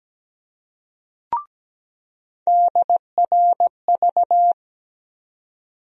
Morse code, heard as a keyed tone, about 17 wpm, 700 Hz. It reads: DRV
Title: USA Call Sign Postfixes - 3 Letters 17wpm